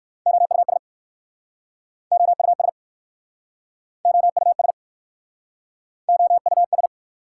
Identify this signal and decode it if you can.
Morse code, heard as a keyed tone, about 59 wpm, 690 Hz. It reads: OUS OUS OUS OUS